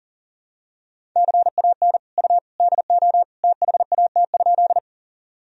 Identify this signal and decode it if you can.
Morse code, heard as a keyed tone, about 40 wpm, 700 Hz. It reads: CAN U DO THAT?
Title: Sentences from Top 100 Words and Abbreviations 40wpm